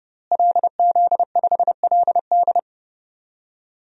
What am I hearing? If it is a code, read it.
Morse code, heard as a keyed tone, about 30 wpm, 700 Hz. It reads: LZ5LD
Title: DXCC-100 Call Signs - 5 Characters - Repeated 2x - 30wpm